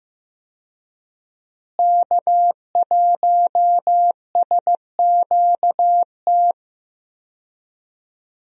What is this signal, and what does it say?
Morse code, heard as a keyed tone, about 15 wpm, 700 Hz. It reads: K1SQT
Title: USA Call Signs - 5 Characters - Repeated 3x - 15wpm